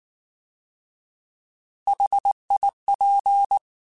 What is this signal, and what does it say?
Morse code, heard as a keyed tone, about 19 wpm, 800 Hz. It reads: HIP